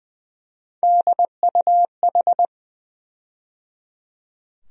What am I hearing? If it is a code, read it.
Morse code, heard as a keyed tone, about 20 wpm, 700 Hz. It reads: DUH
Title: USA Call Sign Postfixes - 3 Letters 20wpm